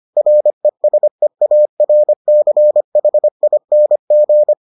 Morse code, heard as a keyed tone, about 25 wpm, 600 Hz. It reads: RESEARCHING